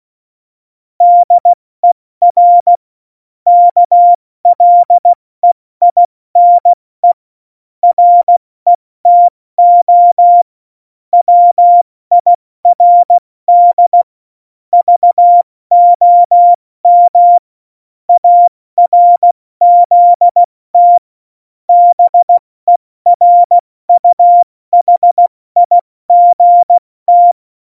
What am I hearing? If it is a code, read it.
Morse code, heard as a keyed tone, about 16 wpm, spaced as for 14 wpm, 700 Hz. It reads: DER KLEINE RETO WIRD VOM ARZT BERUHIGT
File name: Morsekurs / Witze / Witz 19. Dezember 2017, 70 F80